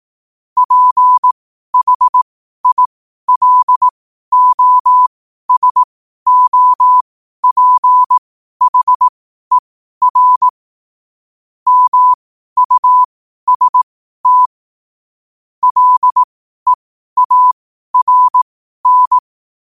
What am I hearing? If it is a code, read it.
Morse code, heard as a keyed tone, about 18 wpm, spaced as for 12 wpm, 1000 Hz. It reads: PHILOSOPHER MUST LEARN
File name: qotdfiles / 12wpm